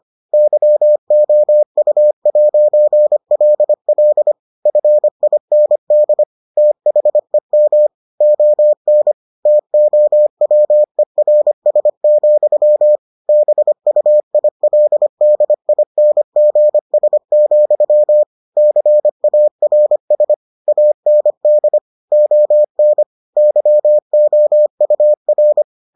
Morse code, heard as a keyed tone, about 25 wpm, 600 Hz. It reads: YOU'LL FIND THEM ON TOWERS, BUILDINGS, CARS AND ON YOUR